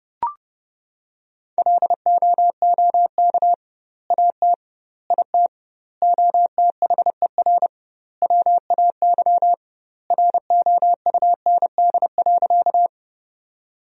Morse code, heard as a keyed tone, about 31 wpm, 700 Hz. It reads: LOOK AT IT OTHER WAY ROUND.
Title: 2x Word-Spacing - Sentences from Top 200 Words - 30wpm